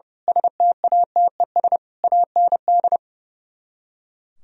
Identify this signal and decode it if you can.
Morse code, heard as a keyed tone, about 28 wpm, 700 Hz. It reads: STATES AND
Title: Speed-Racing - Sets of 5 Words 30wpm